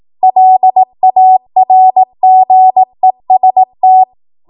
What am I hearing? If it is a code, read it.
Morse code, heard as a keyed tone, about 18 wpm, 750 Hz. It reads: LARGEST
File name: Morse / 18WPM